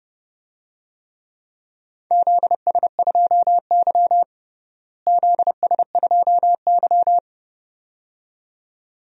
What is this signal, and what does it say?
Morse code, heard as a keyed tone, about 32 wpm, 700 Hz. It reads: ZS2Y ZS2Y
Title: DXCC-100 Call Signs - 4 Characters - Repeated 2x - 30wpm